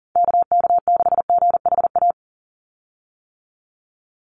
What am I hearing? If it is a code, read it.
Morse code, heard as a keyed tone, about 40 wpm, 700 Hz. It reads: KK6GHA
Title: Call Signs - 40wpm